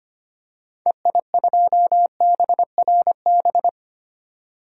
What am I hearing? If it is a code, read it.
Morse code, heard as a keyed tone, about 25 wpm, 700 Hz. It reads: EI2BRB
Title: DXCC-100 Call Signs - 6 Characters - 25wpm